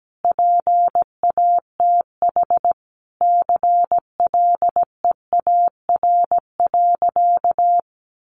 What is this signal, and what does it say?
Morse code, heard as a keyed tone, about 17 wpm, 700 Hz. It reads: PATH CLEAR.
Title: Sentences from Top 1400 Words 17wpm